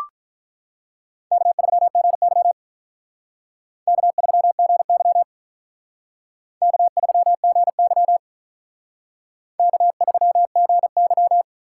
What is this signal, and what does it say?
Morse code, heard as a keyed tone, about 44 wpm, 700 Hz. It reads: K3GY K3GY K3GY K3GY